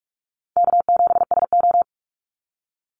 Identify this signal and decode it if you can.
Morse code, heard as a keyed tone, about 44 wpm, 700 Hz. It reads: K7SO
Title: NA SOTA Activator Callsigns 45wpm